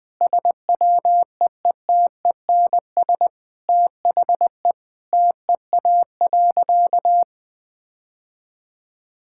Morse code, heard as a keyed tone, about 20 wpm, 700 Hz. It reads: SWEETENS THE TEA.
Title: Sentences with 2-Word Phrases - 20wpm